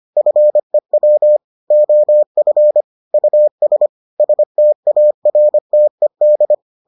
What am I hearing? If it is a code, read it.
Morse code, heard as a keyed tone, about 25 wpm, 600 Hz. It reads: FEW OF US STARTED